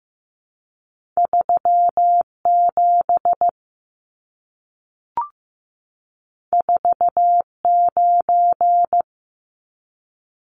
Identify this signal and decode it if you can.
Morse code, heard as a keyed tone, about 15 wpm, 700 Hz. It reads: 37 49
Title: Numbers - 2-Digit 15wpm